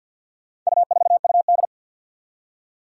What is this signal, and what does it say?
Morse code, heard as a keyed tone, about 51 wpm, 700 Hz. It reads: U4UD
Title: Four Challenging Letter-Number Combinations 50wpm